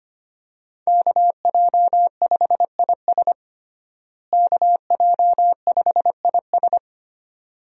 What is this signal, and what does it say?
Morse code, heard as a keyed tone, about 25 wpm, 700 Hz. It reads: KJ5IS KJ5IS
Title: USA Call Signs - 5 Characters - Repeated 2x - 25wpm